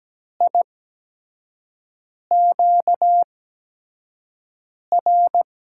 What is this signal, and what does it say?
Morse code, heard as a keyed tone, about 17 wpm, 700 Hz. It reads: I Q R
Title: USA Call Signs - 8x Character Spacing - 17wpm